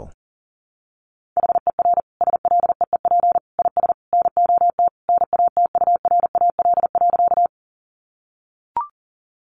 Morse code, heard as a keyed tone, about 40 wpm, 700 Hz. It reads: HER SLEEP IS NOT NATURAL.
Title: Sentences from Top 700 Words 40wpm